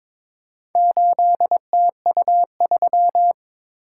Morse code, heard as a keyed tone, about 22 wpm, 690 Hz. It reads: 8TU3
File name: practice / 22wpm / Four Letter-Number Combinations - 22wpm